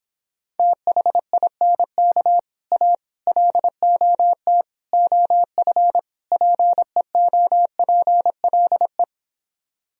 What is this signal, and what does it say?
Morse code, heard as a keyed tone, about 26 wpm, 700 Hz. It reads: THINK A LOT OF PEOPLE